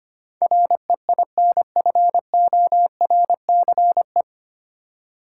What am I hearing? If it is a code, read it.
Morse code, heard as a keyed tone, about 25 wpm, 700 Hz. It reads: REINFORCE
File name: practice / 25wpm / 9 Letter Words - First 3 Letters Given - 25wpm